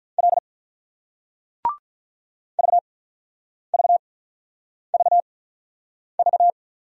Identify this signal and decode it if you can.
Morse code, heard as a keyed tone, about 48 wpm, spaced as for 9 wpm, 700 Hz. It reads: L VVVV